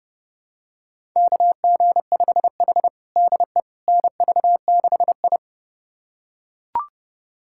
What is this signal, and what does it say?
Morse code, heard as a keyed tone, about 32 wpm, 700 Hz. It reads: KG5H DE NV6I
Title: USA Callsign DE Callsign - 4 Characters - 30wpm